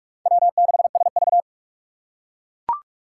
Morse code, heard as a keyed tone, about 44 wpm, 700 Hz. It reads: W6SV